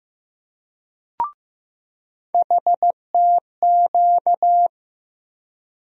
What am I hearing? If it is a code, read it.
Morse code, heard as a keyed tone, about 15 wpm, 700 Hz. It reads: HTQ
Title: USA Call Sign Postfixes - 3 Letters 15wpm